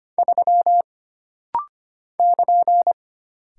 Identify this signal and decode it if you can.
Morse code, heard as a keyed tone, about 25 wpm, 700 Hz. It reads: 3 <KN>
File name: Single Character (Letter:Number:Punctuation:Prosign) - Rapid-Fire - 25wpm